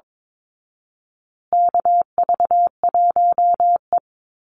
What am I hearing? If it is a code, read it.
Morse code, heard as a keyed tone, about 22 wpm, 700 Hz. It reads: KV1E
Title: Call Signs Repeated 2x - 22wpm